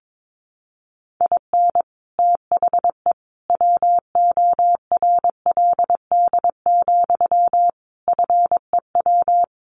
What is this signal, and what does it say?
Morse code, heard as a keyed tone, about 22 wpm, 700 Hz. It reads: IN THE WORLD, FEW
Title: Sentences from Top 1400 Words 22wpm